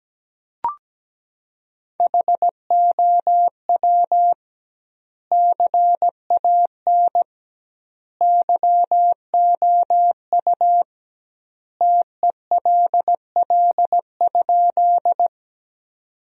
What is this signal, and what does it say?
Morse code, heard as a keyed tone, about 17 wpm, 700 Hz. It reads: HOW CAN YOU TELL?
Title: 2x Word-Spacing - Sentences from Top 200 Words - 17wpm